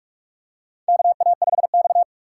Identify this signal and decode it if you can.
Morse code, heard as a keyed tone, about 46 wpm, 700 Hz. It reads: KA5X
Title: USA Call Signs - 6x Character Spacing - 45wpm